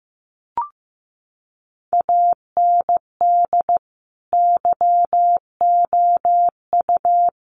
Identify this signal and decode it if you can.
Morse code, heard as a keyed tone, about 15 wpm, 700 Hz. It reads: AND YOU
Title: Sentences from Top 1400 Words 15wpm